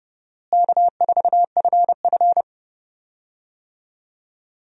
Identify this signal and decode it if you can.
Morse code, heard as a keyed tone, about 29 wpm, 700 Hz. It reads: K4FF
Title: USA Call Signs - 4 Characters - Repeated 3x - 30wpm